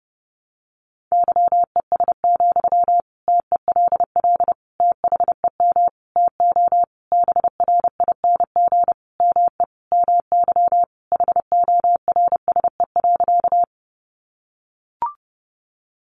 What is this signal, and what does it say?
Morse code, heard as a keyed tone, about 30 wpm, 700 Hz. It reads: YES, TELL THEM TO BRING ME MY HORSE.
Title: Sentences from Top 500 Words 30wpm